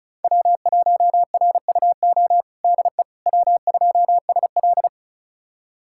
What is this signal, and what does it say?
Morse code, heard as a keyed tone, about 36 wpm, 700 Hz. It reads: W1RUO DE W2SL
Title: USA Callsign DE Callsign 35wpm